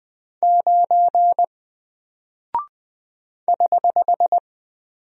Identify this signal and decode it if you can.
Morse code, heard as a keyed tone, about 21 wpm, 700 Hz. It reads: 9 <HH>
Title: Single Character (Letter:Number:Punctuation:Prosign) Alt 20wpm